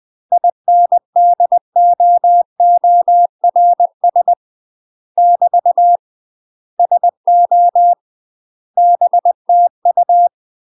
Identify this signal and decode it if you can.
Morse code, heard as a keyed tone, about 20 wpm, 700 Hz. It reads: INDOORS <BT> SO BTU
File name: UK MASTER QSOs TRAINING/20WPM